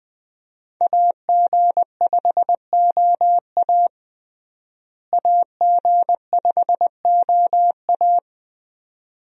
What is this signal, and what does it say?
Morse code, heard as a keyed tone, about 20 wpm, 700 Hz. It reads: AG5OA AG5OA
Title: USA Call Signs - 5 Characters - Repeated 2x - 20wpm